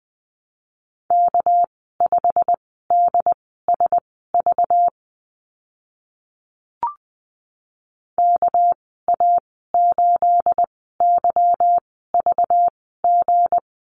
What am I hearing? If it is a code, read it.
Morse code, heard as a keyed tone, about 20 wpm, spaced as for 14 wpm, 700 Hz. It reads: K5DSV KA8YVG
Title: 219 Lesson 35g - Number 0 - Call Signs 20wpm